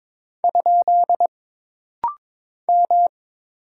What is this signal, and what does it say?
Morse code, heard as a keyed tone, about 22 wpm, 700 Hz. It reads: ? M
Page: This rapid-fire timing is geared to push your ICR (Instant Character Recognition) skills.